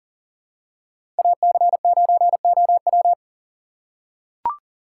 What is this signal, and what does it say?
Morse code, heard as a keyed tone, about 40 wpm, 700 Hz. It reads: AC9OW
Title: USA Call Signs - 6x Character Spacing - 40wpm